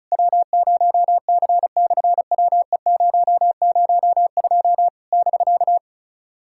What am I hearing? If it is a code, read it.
Morse code, heard as a keyed tone, about 36 wpm, 700 Hz. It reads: W0C/WE002 <BK>
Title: SOTA QSO Practice 35wpm